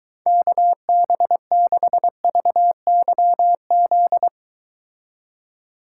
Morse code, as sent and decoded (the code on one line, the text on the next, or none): -.- -... -.... ...- -.-- --..
KB6VYZ